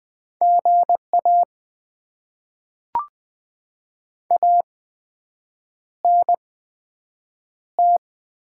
Morse code, heard as a keyed tone, about 20 wpm, spaced as for 7 wpm, 700 Hz. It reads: GA ANT
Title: Top 20 QSO Elements - 8x Character Spacing - 20wpm